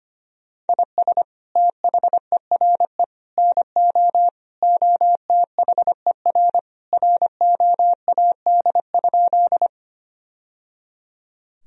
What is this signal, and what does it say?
Morse code, heard as a keyed tone, about 25 wpm, 700 Hz. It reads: IS THERE NO OTHER ROAD?
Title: Sentences from Top 1000 Words - Review Part 2 - 25wpm